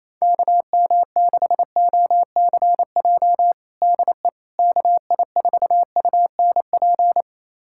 Morse code, heard as a keyed tone, about 28 wpm, 700 Hz. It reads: KM6OCJ DE KI4UNP
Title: USA Callsign DE Callsign - 6 Characters - 28wpm